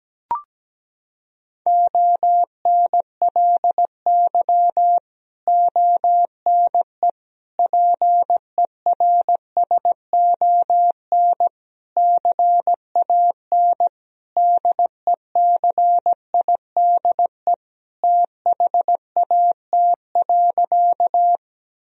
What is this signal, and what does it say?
Morse code, heard as a keyed tone, about 17 wpm, 700 Hz. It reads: ONLY ONE PERSON CAN DECIDE THAT.
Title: Sentences from Top 1000 Words - Review Part 3 - 17wpm